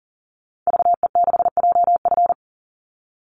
Morse code, heard as a keyed tone, about 40 wpm, 700 Hz. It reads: VE6JF
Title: Call Signs - 40wpm